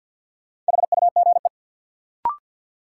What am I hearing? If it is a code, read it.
Morse code, heard as a keyed tone, about 51 wpm, 700 Hz. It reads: HUGE
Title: Sentences from Top 1100 Words - New Word Drill Alt - 50wpm